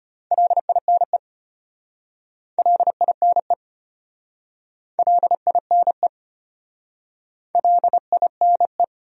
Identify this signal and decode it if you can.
Morse code, heard as a keyed tone, about 33 wpm, 700 Hz. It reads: LINE LINE LINE LINE